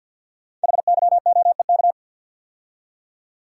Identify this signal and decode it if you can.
Morse code, heard as a keyed tone, about 52 wpm, 700 Hz. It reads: HYOEK
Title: Code Groups - 5 Letters - 50wpm